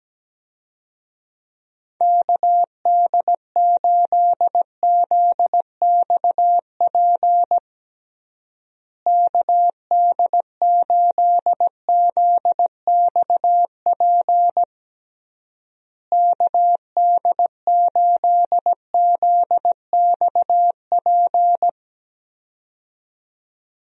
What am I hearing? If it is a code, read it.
Morse code, heard as a keyed tone, about 17 wpm, 700 Hz. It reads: KD8ZXP KD8ZXP KD8ZXP
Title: USA Call Signs - 6 Characters - Repeated 3x - 17wpm